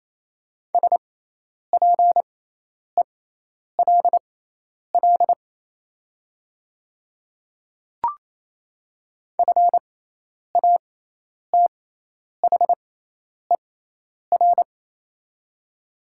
Morse code, heard as a keyed tone, about 28 wpm, spaced as for 10 wpm, 700 Hz. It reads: SPELL FATHER
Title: 6x Character Spacing - Top 200 Words - 28wpm